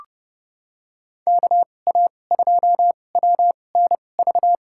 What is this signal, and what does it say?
Morse code, heard as a keyed tone, about 30 wpm, spaced as for 22 wpm, 700 Hz. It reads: KA2WNV
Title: USA Call Signs - 2x Character Spacing - 30wpm